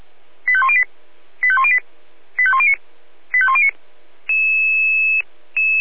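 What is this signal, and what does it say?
Five-tone selective call (ZVEI), heard as quick runs of five tones, tones of about 70 ms. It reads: 85108 85108 85109 85109
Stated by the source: ZVEI five-tone selective call 85107 received by SDR.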